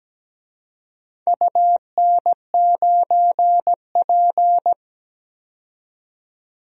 Morse code, heard as a keyed tone, about 17 wpm, 700 Hz. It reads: UN9P
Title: DXCC-100 Call Signs - 4 Characters - 17wpm